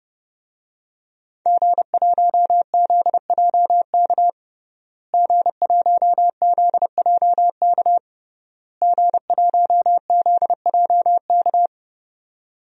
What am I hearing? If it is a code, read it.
Morse code, heard as a keyed tone, about 30 wpm, 700 Hz. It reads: G1ZJK G1ZJK G1ZJK
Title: DXCC-100 Call Signs - 5 Characters - Repeated 3x - 30wpm